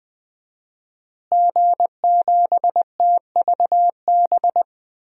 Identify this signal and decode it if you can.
Morse code, heard as a keyed tone, about 20 wpm, 700 Hz. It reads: G7TVB
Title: DXCC-100 Call Signs - 5 Characters - Repeated 3x - 20wpm